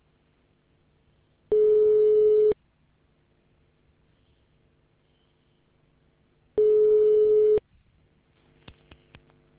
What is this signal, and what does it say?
Ringback tone (Europe, 425 Hz).